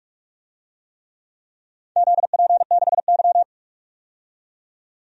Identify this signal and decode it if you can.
Morse code, heard as a keyed tone, about 47 wpm, 700 Hz. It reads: ZP6Y
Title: DXCC-100 Call Signs - 4 Characters - 45wpm